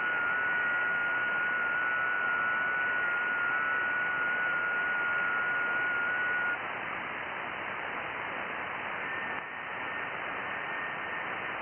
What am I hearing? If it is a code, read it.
FT8.
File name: IC-7100